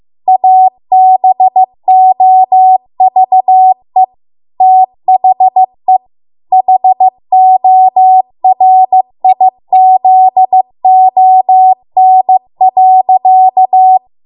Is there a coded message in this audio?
Yes: Morse code from 0.3 to 14.0 s, heard as a keyed tone, about 15 wpm, 750 Hz. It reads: ABOVE THE HORIZON.